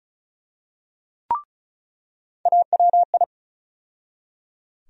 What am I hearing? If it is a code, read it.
Morse code, heard as a keyed tone, about 34 wpm, 700 Hz. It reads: AWI
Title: USA Call Sign Postfixes - 3 Letters 35wpm